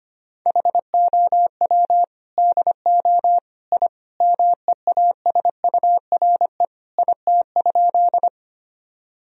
Morse code, heard as a keyed tone, about 25 wpm, 700 Hz. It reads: HOW DO I MEASURE IT?
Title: Sentences from Top 1000 Words - Review Part 2 - 25wpm